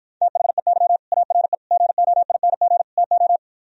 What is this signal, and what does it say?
Morse code, heard as a keyed tone, about 54 wpm, 700 Hz. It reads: THEY ARE GOING TO